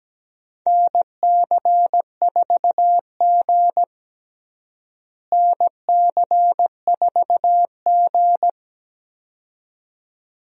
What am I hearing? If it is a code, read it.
Morse code, heard as a keyed tone, about 17 wpm, 700 Hz. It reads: NC4G NC4G
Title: USA Call Signs - 4 Characters - Repeated 2x - 17wpm